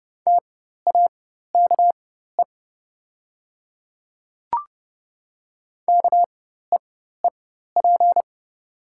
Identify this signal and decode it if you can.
Morse code, heard as a keyed tone, about 30 wpm, spaced as for 14 wpm, 700 Hz. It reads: TAKE KEEP
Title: 4x Character Spacing - Top 500 Words - 30wpm